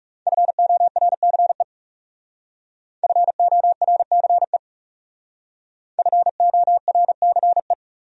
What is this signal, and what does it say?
Morse code, heard as a keyed tone, about 38 wpm, 700 Hz. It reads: FORCE FORCE FORCE